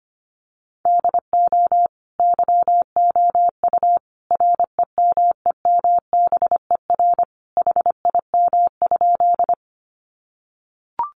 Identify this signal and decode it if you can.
Morse code, heard as a keyed tone, about 25 wpm, 700 Hz. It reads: DO YOU REMEMBER HIM?
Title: Sentences from Top 400 Words 25wpm